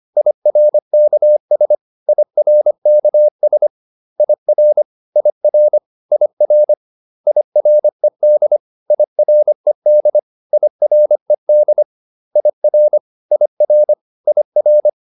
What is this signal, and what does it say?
Morse code, heard as a keyed tone, about 25 wpm, 600 Hz. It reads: IRKS IRKS IR IR IR IRED IRED IRED IR IR IR